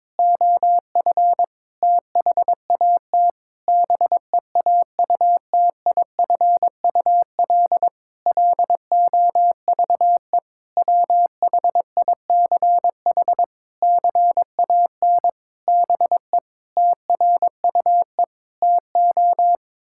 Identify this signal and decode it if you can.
Morse code, heard as a keyed tone, about 22 wpm, 700 Hz. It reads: OF THAT BEAUTIFUL LOVE WHICH CAN BE TRUE TO
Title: Sentences from Top 1600 Words 22wpm